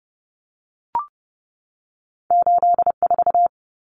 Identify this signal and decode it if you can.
Morse code, heard as a keyed tone, about 30 wpm, 700 Hz. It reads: 84
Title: Common Cut Numbers - 2-Digit 30wpm